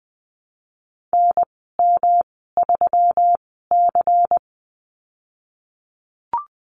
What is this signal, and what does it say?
Morse code, heard as a keyed tone, about 20 wpm, spaced as for 15 wpm, 700 Hz. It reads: NM3C
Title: 154 Lesson 25g - Letter p - Call Signs 20wpm